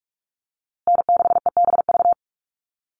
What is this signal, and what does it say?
Morse code, heard as a keyed tone, about 44 wpm, 690 Hz. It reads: N6EBV
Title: USA Call Signs Repeated 3x 45wpm